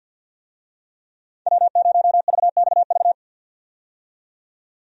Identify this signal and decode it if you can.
Morse code, heard as a keyed tone, about 50 wpm, 700 Hz. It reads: W0VXV
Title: USA Call Signs - 5 Characters - 50wpm